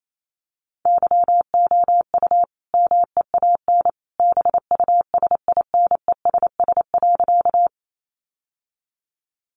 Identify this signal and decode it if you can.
Morse code, heard as a keyed tone, about 28 wpm, 700 Hz. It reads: YOU MEAN BUSINESS.
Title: Sentences from Top 600 Words 28wpm